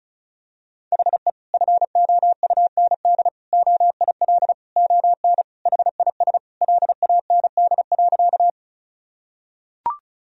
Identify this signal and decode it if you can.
Morse code, heard as a keyed tone, about 36 wpm, 700 Hz. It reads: HE FOUND OIL ON HIS LAND.
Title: Sentences from Top 1000 Words - Review Part 4 - 35wpm